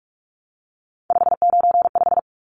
Morse code, heard as a keyed tone, about 44 wpm, 700 Hz. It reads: 595
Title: Numbers - 3-Digit 45wpm